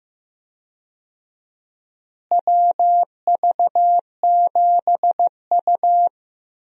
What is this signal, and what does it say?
Morse code, heard as a keyed tone, about 15 wpm, 700 Hz. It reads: WV7U
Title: USA Call Signs - 4 Characters - Repeated 3x - 15wpm